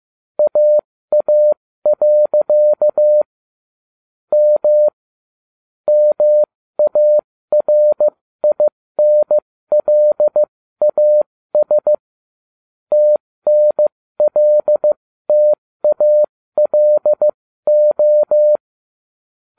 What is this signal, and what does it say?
Morse code, heard as a keyed tone, about 15 wpm, spaced as for 13 wpm, 600 Hz. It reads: AA. M MARINLAS TNLTALO
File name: cw / ubacw13 2